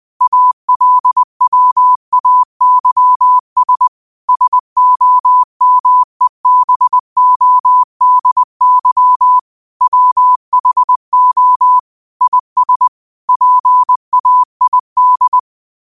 Morse code, heard as a keyed tone, about 20 wpm, 1000 Hz. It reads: ALWAYS SOMEBODY WHO IS PAID